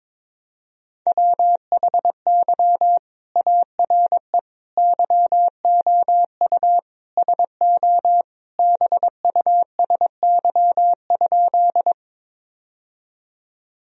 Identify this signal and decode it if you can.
Morse code, heard as a keyed tone, about 21 wpm, 700 Hz. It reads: WHY ARE YOU SO BUSY?
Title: Sentences from Top 1000 Words - Review Part 3 - 22wpm